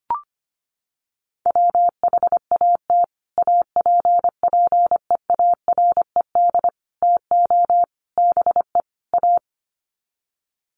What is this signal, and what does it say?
Morse code, heard as a keyed tone, about 25 wpm, 700 Hz. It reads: WHAT APPEARED TO BE A
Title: Sets of 5 Words Encore2 25wpm